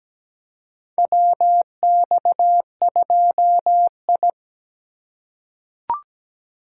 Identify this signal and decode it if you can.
Morse code, heard as a keyed tone, about 17 wpm, 700 Hz. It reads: WX2I
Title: USA Call Signs - 4 Characters - 2x Character Spacing - 17wpm